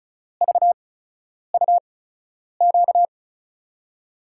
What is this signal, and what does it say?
Morse code, heard as a keyed tone, about 36 wpm, spaced as for 10 wpm, 700 Hz. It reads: VUQ
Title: USA Call Signs - 8x Character Spacing - 35wpm